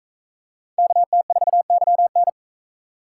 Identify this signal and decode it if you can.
Morse code, heard as a keyed tone, about 40 wpm, 700 Hz. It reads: KT4YN